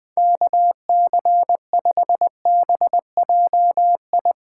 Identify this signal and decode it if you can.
Morse code, heard as a keyed tone, about 21 wpm, 700 Hz. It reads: KC5BJI
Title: USA Callsign DE Callsign - 6 Characters - 20wpm